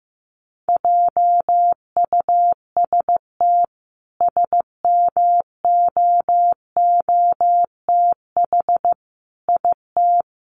Morse code, heard as a keyed tone, about 15 wpm, 700 Hz. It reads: JUST SMOOTH IT
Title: Sentences from Top 1400 Words 15wpm